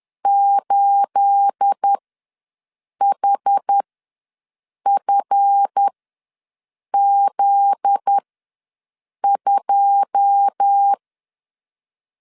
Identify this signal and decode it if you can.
Morse code, heard as a keyed tone, about 10 wpm, spaced as for 6 wpm, 800 Hz. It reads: 8HFZ2
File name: LEZIONI / 9 28 ws 10wpm 4124